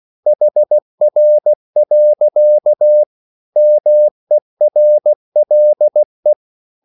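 Morse code, heard as a keyed tone, about 16 wpm, 600 Hz. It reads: HR. MERLE
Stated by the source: Slow (QRS) Morse code at 16 wpm, from CW med Gnister.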